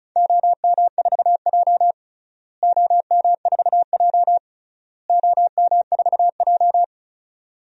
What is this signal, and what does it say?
Morse code, heard as a keyed tone, about 36 wpm, 690 Hz. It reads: OM4J OM4J OM4J
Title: DXCC-100 Call Signs - 4 Characters - Repeated 3x - 35wpm